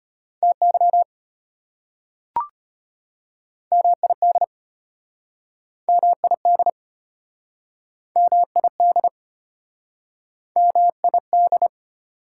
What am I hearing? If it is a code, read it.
Morse code, heard as a keyed tone, about 33 wpm, 700 Hz. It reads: TY MID MID MID MID